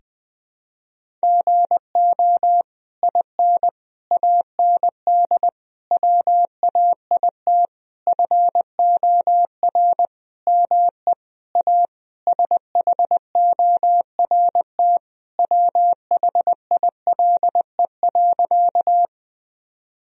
Morse code, heard as a keyed tone, about 20 wpm, 700 Hz. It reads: GO IN AND WAIT FOR ME A SHORT WHILE.
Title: Sentences from Top 1000 Words - Review Part 2 - 20wpm